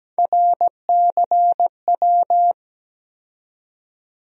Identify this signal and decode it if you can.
Morse code, heard as a keyed tone, about 17 wpm, 700 Hz. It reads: RCW
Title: USA Call Sign Postfixes - 3 Letters 17wpm